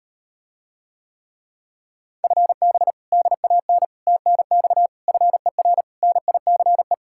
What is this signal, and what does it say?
Morse code, heard as a keyed tone, about 38 wpm, 700 Hz. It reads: FB DAN TNX FER NICE